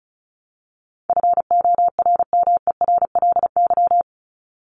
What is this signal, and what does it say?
Morse code, heard as a keyed tone, about 34 wpm, 700 Hz. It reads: FORMERLY
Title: LY Suffix - Top 100 Words with Base Words 35wpm